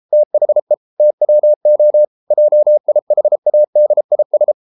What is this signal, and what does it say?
Morse code, heard as a keyed tone, about 33 wpm, 600 Hz. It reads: THE TWO JIHADIS